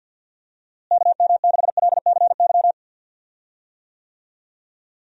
Morse code, heard as a keyed tone, about 45 wpm, 700 Hz. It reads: KM6LCY
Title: USA Call Signs - 6 Characters - Repeated 2x - 50wpm